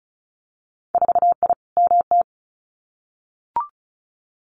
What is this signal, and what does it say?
Morse code, heard as a keyed tone, about 36 wpm, 700 Hz. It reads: 4I MT
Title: Winter Field Day Exchanges Encore 35wpm